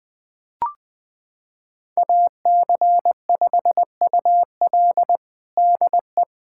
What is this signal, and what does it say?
Morse code, heard as a keyed tone, about 20 wpm, 700 Hz. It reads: AC5UL DE
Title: USA Callsign DE Callsign 20wpm